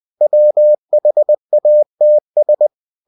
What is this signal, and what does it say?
Morse code, heard as a keyed tone, about 20 wpm, 600 Hz. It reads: WHATS
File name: cw-books / winnie-the-pooh / winnie-20wpm-page-0110